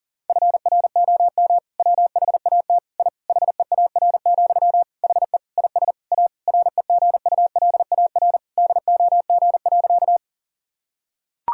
Morse code, heard as a keyed tone, about 40 wpm, 700 Hz. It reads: FROM WHAT I HEAR, HE IS A REGULAR DOG.
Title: Sentences from Top 1000 Words - Review Part 6 - 40wpm